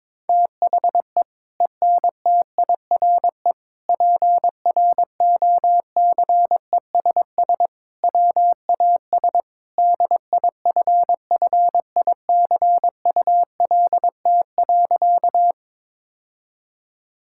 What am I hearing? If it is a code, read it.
Morse code, heard as a keyed tone, about 22 wpm, 700 Hz. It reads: THE ENTIRE PROCESS WAS DIFFICULT.
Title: Sentences from Top 1000 Words - Review Part 5 - 22wpm